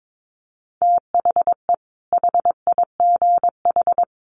Morse code, heard as a keyed tone, about 22 wpm, 700 Hz. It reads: THE HIGH
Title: Sentences from Top 1300 Words 22wpm